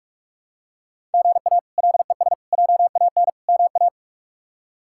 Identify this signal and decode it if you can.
Morse code, heard as a keyed tone, about 46 wpm, 700 Hz. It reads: GA LES JAN MA
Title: K1USN SST QSO Practice - 45wpm